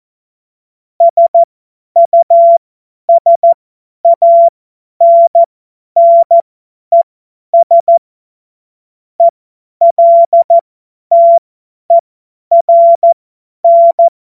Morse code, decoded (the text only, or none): SUSANNES ELTERN